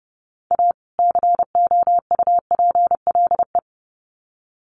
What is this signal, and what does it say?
Morse code, heard as a keyed tone, about 28 wpm, 700 Hz. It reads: A COUPLE
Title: Sets of 2 Words 30wpm